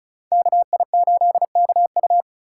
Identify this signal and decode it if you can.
Morse code, heard as a keyed tone, about 36 wpm, 700 Hz. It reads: KI8KU
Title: USA Call Signs - 5 Characters - 3x Character Spacing - 35wpm